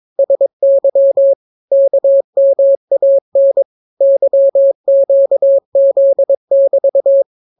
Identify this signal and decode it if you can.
Morse code, heard as a keyed tone, about 22 wpm, 550 Hz. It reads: SY KMAN YQZ<BT>